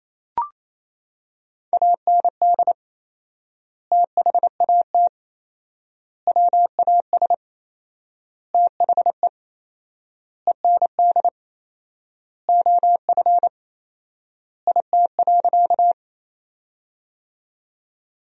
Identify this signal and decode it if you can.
Morse code, heard as a keyed tone, about 28 wpm, 700 Hz. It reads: AND THAT WAS THE END OF IT.
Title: 4x Word-Spacing - Sentences from Top 200 Words - 28wpm